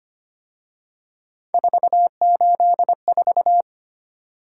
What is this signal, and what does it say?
Morse code, heard as a keyed tone, about 29 wpm, 700 Hz. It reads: 484
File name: Speed-Racing - Three Letter-Number Combinations - 25wpm